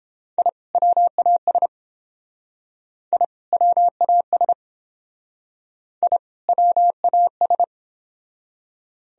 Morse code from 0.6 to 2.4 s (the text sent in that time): WAS